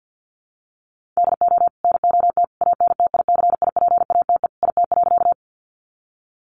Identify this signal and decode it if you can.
Morse code, heard as a keyed tone, about 50 wpm, 700 Hz. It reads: DO NOT ANTICIPATE IT.
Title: Sentences from Top 1500 Words 50wpm